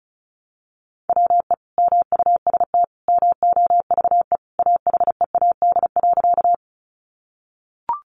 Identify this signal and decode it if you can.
Morse code, heard as a keyed tone, about 34 wpm, 700 Hz. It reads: WE MUST MOVE AHEAD.
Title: Sentences from Top 1100 Words 35wpm